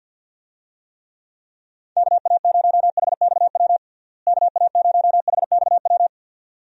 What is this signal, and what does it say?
Morse code, heard as a keyed tone, about 46 wpm, 700 Hz. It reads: KA0HXW KA0HXW
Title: USA Call Signs - 6 Characters - Repeated 2x - 50wpm